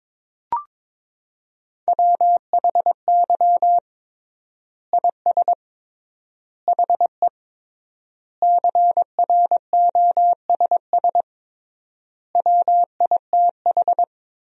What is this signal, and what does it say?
Morse code, heard as a keyed tone, about 22 wpm, 700 Hz. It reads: WHY IS HE CROSS WITH
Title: 3x Word-Spacing - Sentences from Top 300 Words - 22wpm